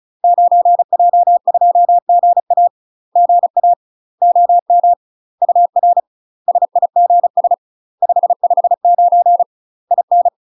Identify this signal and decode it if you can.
Morse code, heard as a keyed tone, about 35 wpm, 700 Hz. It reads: 9J2GA GA OM UR SIGS 559 IN
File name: UK MASTER QSOs TRAINING/35WPM